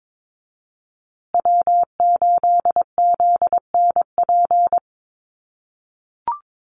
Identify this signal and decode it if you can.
Morse code, heard as a keyed tone, about 22 wpm, 700 Hz. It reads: W8ZNP
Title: USA Call Signs Repeated 3x 22wpm